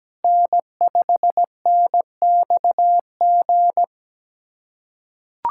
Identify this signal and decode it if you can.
Morse code, heard as a keyed tone, about 17 wpm, 700 Hz. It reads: N5NXG
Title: USA Callsign DE Callsign - 5 Characters - 17wpm